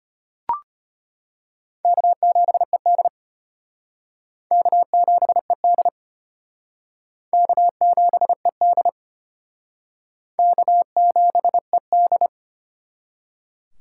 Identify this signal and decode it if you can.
Morse code, heard as a keyed tone, about 31 wpm, 700 Hz. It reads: K7ED K7ED K7ED K7ED
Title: Speed Racing - USA 1x2 Call Signs 25wpm